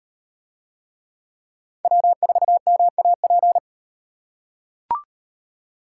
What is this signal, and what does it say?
Morse code, heard as a keyed tone, about 36 wpm, 700 Hz. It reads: W4MAP